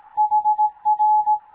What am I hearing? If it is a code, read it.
Morse code, heard as a keyed tone, about 18 wpm, 850 Hz. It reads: HR
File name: Morze-HA2MN / hangp / 2 11 GYAKORLAT